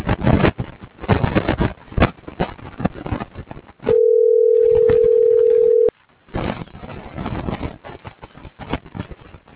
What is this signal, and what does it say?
Ringback tone (North America), 440 + 480 Hz.